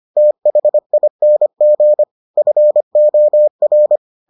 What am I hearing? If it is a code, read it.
Morse code, heard as a keyed tone, about 25 wpm, 600 Hz. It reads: THING FOR